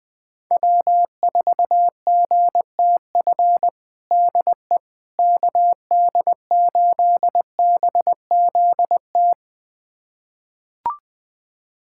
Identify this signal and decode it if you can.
Morse code, heard as a keyed tone, about 21 wpm, 700 Hz. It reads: W4GTF DE KD8BZT
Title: USA Callsign DE Callsign 20wpm